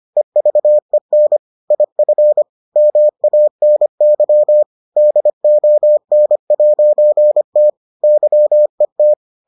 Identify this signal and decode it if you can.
Morse code, heard as a keyed tone, about 25 wpm, 600 Hz. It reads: EVEN IF MANY DON'T YET